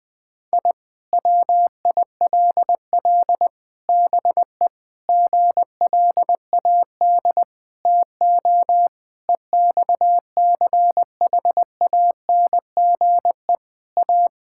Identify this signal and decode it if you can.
Morse code, heard as a keyed tone, about 20 wpm, 700 Hz. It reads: I WILL BE GLAD TO EXCHANGE A
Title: Sentences from Top 1000 Words - Review Part 6 - 20wpm